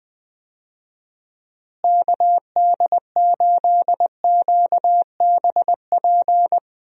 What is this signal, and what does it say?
Morse code, heard as a keyed tone, about 19 wpm, 700 Hz. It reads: KD8QBP
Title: USA Call Signs - 6 Characters - Repeated 3x - 20wpm